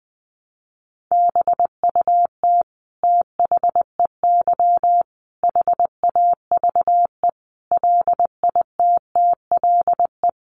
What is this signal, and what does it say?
Morse code, heard as a keyed tone, about 20 wpm, 700 Hz. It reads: BUT THEY HAVE LITTLE
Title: Sentences from Top 1200 Words 20wpm